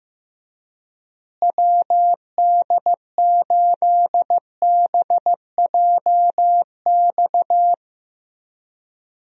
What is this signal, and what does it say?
Morse code, heard as a keyed tone, about 15 wpm, 700 Hz. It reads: WD8BJX
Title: USA Call Signs - 6 Characters - Repeated 3x - 15wpm